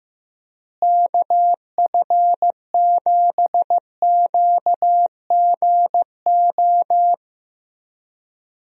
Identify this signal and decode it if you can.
Morse code, heard as a keyed tone, about 15 wpm, 700 Hz. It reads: KF7QGO
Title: USA Call Signs - 6 Characters - Repeated 2x - 15wpm